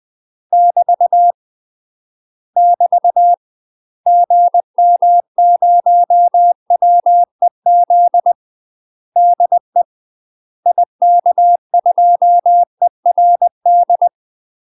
Morse code, heard as a keyed tone, about 20 wpm, 700 Hz. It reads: <BT> <BT> GM0WEZ DE IK2ERD
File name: UK MASTER QSOs TRAINING / 20WPM / UK MASTER QSOs 0008